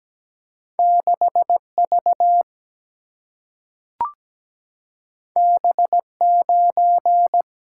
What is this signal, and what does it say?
Morse code, heard as a keyed tone, about 17 wpm, 700 Hz. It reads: 6V B9
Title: Single-Letter Single-Number 17wpm